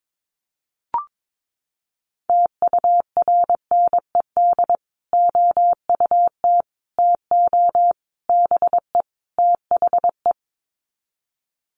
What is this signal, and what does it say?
Morse code, heard as a keyed tone, about 22 wpm, 700 Hz. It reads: TURNED OUT TO BE THE
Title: Sets of 5 Words Encore 22wpm